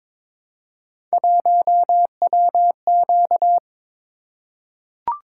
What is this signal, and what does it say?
Morse code, heard as a keyed tone, about 23 wpm, 700 Hz. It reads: 1WQ
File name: practice / 22wpm / Three Letter-Number Combinations - 22wpm